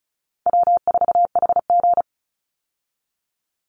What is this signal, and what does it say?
Morse code, heard as a keyed tone, about 34 wpm, 700 Hz. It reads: W4HG
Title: Call Signs Repeated 2x - 35wpm